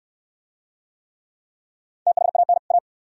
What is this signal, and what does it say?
Morse code, heard as a keyed tone, about 66 wpm, 700 Hz. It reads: THAN A